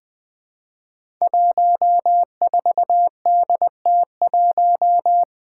Morse code, heard as a keyed tone, about 20 wpm, 700 Hz. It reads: 14DT1
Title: Five Challenging Letter-Number Combinations 20wpm